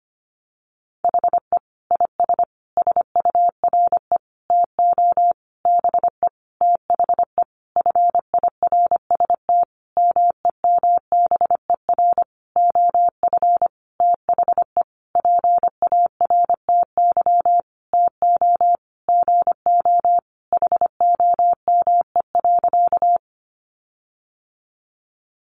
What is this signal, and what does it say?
Morse code, heard as a keyed tone, about 25 wpm, 700 Hz. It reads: HE IS SURE TO BE THE FIRST MEMBER OF THE PARTY TO GO HOME.
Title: Sentences from Top 1400 Words 25wpm